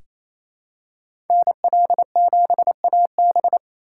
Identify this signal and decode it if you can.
Morse code, heard as a keyed tone, about 28 wpm, 700 Hz. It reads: NL7AB
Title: USA Call Signs - 3x Character Spacing - 28wpm